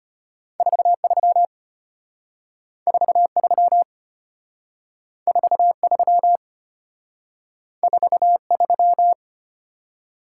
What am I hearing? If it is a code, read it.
Morse code, heard as a keyed tone, about 32 wpm, 700 Hz. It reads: 43 43 43 43